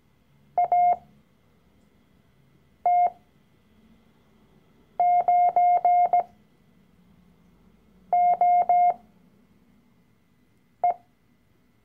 Morse code, heard as a keyed tone, about 17 wpm, spaced as for 4 wpm, 710 Hz. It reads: AT9OE
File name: CW-Online-Kurs / Anfänger / Lektion 012